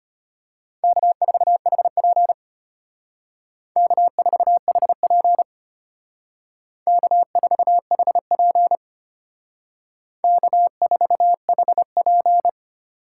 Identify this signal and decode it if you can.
Morse code, heard as a keyed tone, about 32 wpm, 700 Hz. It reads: K4HP K4HP K4HP K4HP